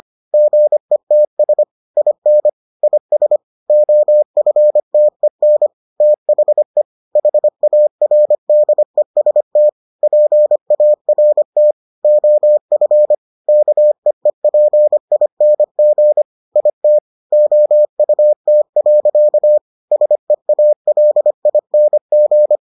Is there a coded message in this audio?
Yes: Morse code from 0.3 to 22.6 s, heard as a keyed tone, about 25 wpm, 600 Hz. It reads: GETS IN IS OFTEN THE HARDEST PART OF KEEPING IT OUT. SEALING